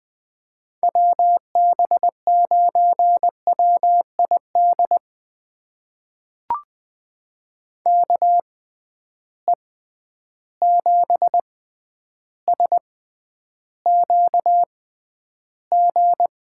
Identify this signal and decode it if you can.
Morse code, heard as a keyed tone, about 20 wpm, spaced as for 10 wpm, 700 Hz. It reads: WB9WID KE7SQG
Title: USA Call Signs - 6 Characters - 6x Character Spacing - 20wpm